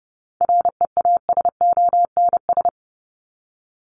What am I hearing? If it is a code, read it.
Morse code, heard as a keyed tone, about 30 wpm, 700 Hz. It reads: REASONS
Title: ES:S Suffix - Top 100 Words with Base Words 30wpm